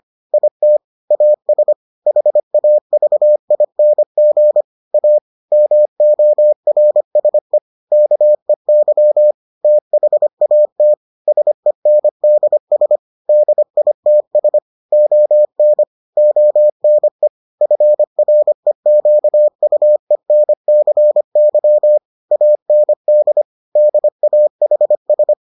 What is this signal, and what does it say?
Morse code, heard as a keyed tone, about 25 wpm, 600 Hz. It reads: IT AS HAVING A MORSE KEY THAT SENDS DITS ON ONE FREQUENCY AND DAHS